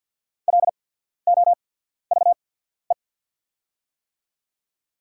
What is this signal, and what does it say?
Morse code, heard as a keyed tone, about 54 wpm, spaced as for 14 wpm, 690 Hz. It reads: LOVE
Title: 8x Character Spacing - Top 500 Words - 50wpm